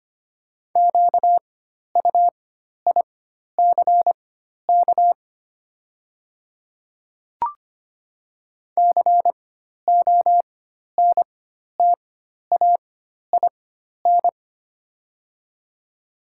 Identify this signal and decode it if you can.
Morse code, heard as a keyed tone, about 25 wpm, spaced as for 12 wpm, 700 Hz. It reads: QUICK CONTAIN
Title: 4x Character Spacing - Top 500 Words - 25wpm